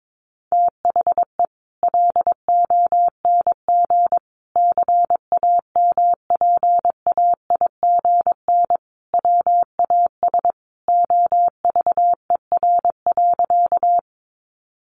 Morse code, heard as a keyed tone, about 22 wpm, 700 Hz. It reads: THE LONG CAMPAIGN WAS OVER.
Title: Sentences from Top 1200 Words 22wpm